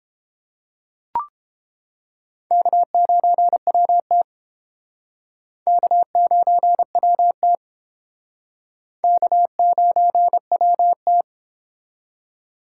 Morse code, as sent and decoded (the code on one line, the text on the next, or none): -.- ----. .-- - / -.- ----. .-- - / -.- ----. .-- -
K9WT K9WT K9WT